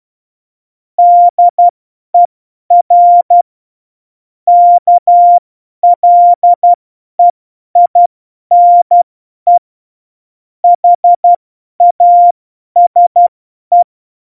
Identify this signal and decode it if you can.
Morse code, heard as a keyed tone, about 12 wpm, spaced as for 10 wpm, 700 Hz. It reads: DER KLEINE HASE
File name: Morsekurs / Witze / Witz 03 April 2018, 50 F70